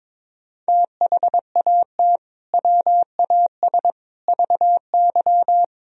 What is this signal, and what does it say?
Morse code, heard as a keyed tone, about 21 wpm, 700 Hz. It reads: THAT WAS VY
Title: Sentences from Top 200 Words and Abbreviations 22wpm